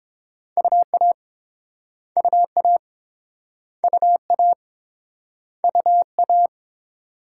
Morse code, heard as a keyed tone, about 30 wpm, 700 Hz. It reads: UA UA UA UA